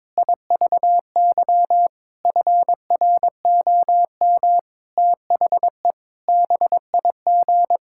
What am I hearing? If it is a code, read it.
Morse code, heard as a keyed tone, about 22 wpm, 700 Hz. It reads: IVY FROM THE BIG
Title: Pangram Challenge v2 - 22wpm